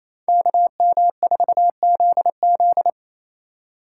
Morse code, heard as a keyed tone, about 28 wpm, 700 Hz. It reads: KM4ZZ
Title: K1USN SST QSO Practice - 28wpm